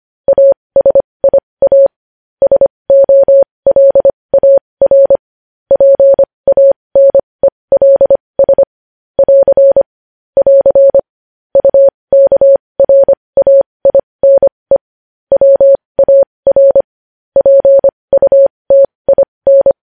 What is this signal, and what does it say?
Morse code, heard as a keyed tone, about 25 wpm, spaced as for 20 wpm, 550 Hz. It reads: ASIA SOLAR PANELS <AR> <AR> UKRAINE WAR PUTIN
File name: podcast / News headlines-1x-25x20